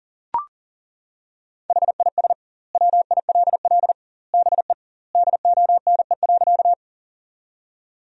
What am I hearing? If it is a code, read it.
Morse code, heard as a keyed tone, about 37 wpm, 700 Hz. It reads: HIS WILL BE DONE.
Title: 2x Word-Spacing - Sentences from Top 500 Words - 40wpm